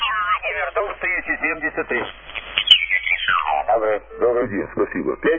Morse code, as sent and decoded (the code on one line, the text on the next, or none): none